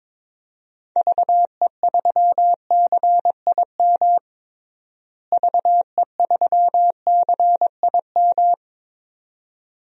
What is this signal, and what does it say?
Morse code, heard as a keyed tone, about 22 wpm, 700 Hz. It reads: VE3CIM VE3CIM
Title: DXCC-100 Call Signs - 6 Characters - Repeated 2x - 22wpm